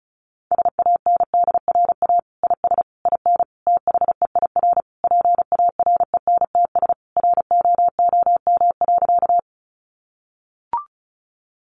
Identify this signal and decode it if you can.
Morse code, heard as a keyed tone, about 34 wpm, 700 Hz. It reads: SANDRA IS IN THEIR PARENTS ROOM.
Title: Easy Phrases 35wpm